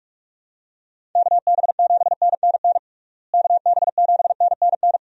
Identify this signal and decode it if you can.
Morse code, heard as a keyed tone, about 46 wpm, 700 Hz. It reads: KB7NNN KB7NNN
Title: USA Call Signs - 6 Characters - Repeated 2x - 45wpm